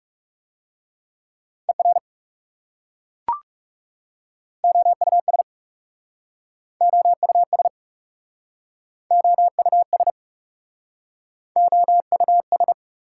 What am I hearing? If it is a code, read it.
Morse code, heard as a keyed tone, about 40 wpm, 690 Hz. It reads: ER OUS OUS OUS OUS